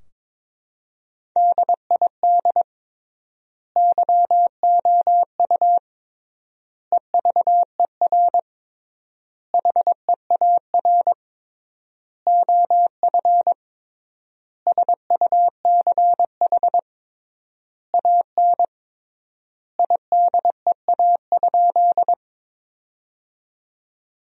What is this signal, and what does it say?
Morse code, heard as a keyed tone, about 22 wpm, 700 Hz. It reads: DID YOU EVER HEAR OF SUCH AN IDEA?
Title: 3x Word-Spacing - Sentences from Top 400 Words - 22wpm